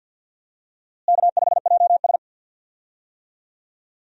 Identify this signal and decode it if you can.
Morse code, heard as a keyed tone, about 46 wpm, 700 Hz. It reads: K5JS
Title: USA Call Signs - 4 Characters - Repeated 3x - 50wpm